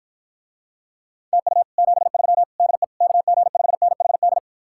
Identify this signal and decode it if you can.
Morse code, heard as a keyed tone, about 54 wpm, 700 Hz. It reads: TU 73 DE KG5NHD